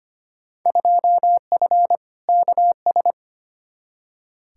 Morse code, heard as a keyed tone, about 25 wpm, 700 Hz. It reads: 2F KS
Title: Field Day Exchanges - 25wpm